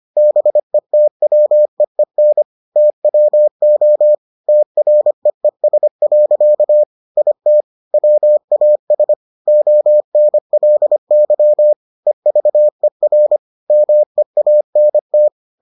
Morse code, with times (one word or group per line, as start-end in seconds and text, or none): 0.2-2.4 BETWEEN
2.8-4.1 TWO
4.5-6.8 TREES.
7.2-7.6 IT
7.9-9.1 WAS
9.5-11.7 ONLY
12.1-13.4 EVER
13.7-15.3 MEANT